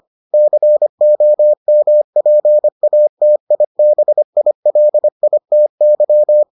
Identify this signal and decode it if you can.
Morse code, heard as a keyed tone, about 25 wpm, 600 Hz. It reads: COMPATIBILITY